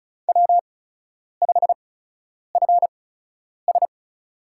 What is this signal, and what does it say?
Morse code, heard as a keyed tone, about 36 wpm, spaced as for 10 wpm, 700 Hz. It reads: W5FS
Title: USA Call Signs - 8x Character Spacing - 35wpm